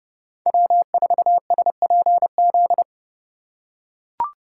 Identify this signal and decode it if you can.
Morse code, heard as a keyed tone, about 30 wpm, 700 Hz. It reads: W4SPZ
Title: USA Call Signs - 6x Character Spacing - 30wpm